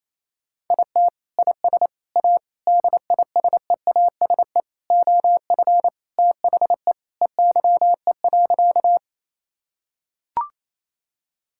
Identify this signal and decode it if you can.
Morse code, heard as a keyed tone, about 28 wpm, 700 Hz. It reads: IT IS A DISEASE OF THE EYE.
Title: Sentences from Top 1000 Words - Review Part 6 - 28wpm